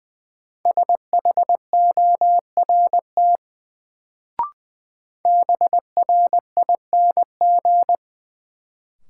Morse code, heard as a keyed tone, about 21 wpm, 700 Hz. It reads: SHORT BRING
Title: Sets of 5 Words Encore - New Word Drill - 20wpm